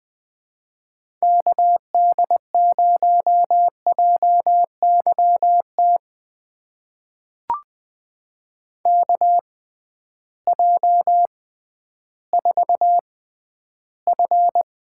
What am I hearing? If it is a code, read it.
Morse code, heard as a keyed tone, about 21 wpm, spaced as for 12 wpm, 700 Hz. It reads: KD0JYT KJ4F
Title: USA Call Signs - 6 Characters - 6x Character Spacing - 20wpm